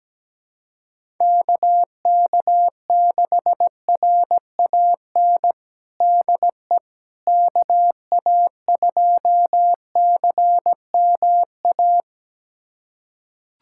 Morse code, heard as a keyed tone, about 17 wpm, 700 Hz. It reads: KK6RAN DE KA2CMA
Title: USA Callsign DE Callsign - 6 Characters - 17wpm